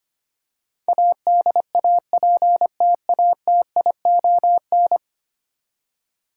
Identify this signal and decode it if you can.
Morse code, heard as a keyed tone, about 25 wpm, 700 Hz. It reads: ADAPTATION
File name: Ten Letter Words - 25wpm